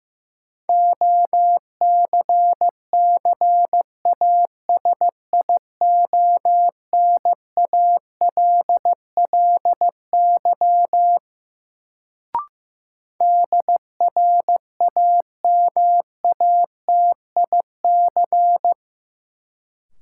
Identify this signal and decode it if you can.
Morse code, heard as a keyed tone, about 15 wpm, 700 Hz. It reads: OCCASIONALLY DRAMATIC
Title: Sentences from Top 1300 Words - New Word Drill Alt - 15wpm